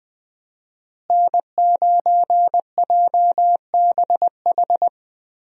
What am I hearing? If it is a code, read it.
Morse code, heard as a keyed tone, about 20 wpm, 700 Hz. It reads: N9JBH
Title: USA Call Signs - 5 Characters - Repeated 2x - 20wpm